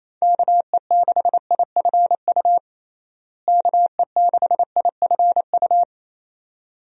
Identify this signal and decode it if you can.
Morse code, heard as a keyed tone, about 28 wpm, 700 Hz. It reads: KE6IFU KE6IFU
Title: USA Call Signs - 6 Characters - Repeated 2x - 28wpm